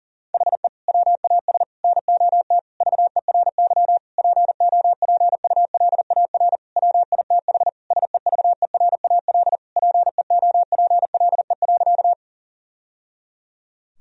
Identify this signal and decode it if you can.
Morse code, heard as a keyed tone, about 37 wpm, 700 Hz. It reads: HE WAS NOT VERY POPULAR WITH SEVERAL PEOPLE.
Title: Sentences from Top 1000 Words - Review Part 3 - 40wpm